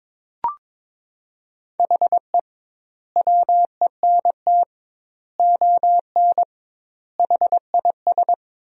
Morse code, heard as a keyed tone, about 22 wpm, 700 Hz. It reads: HE WENT ON HIS
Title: 2x Word-Spacing - Sentences from Top 200 Words - 22wpm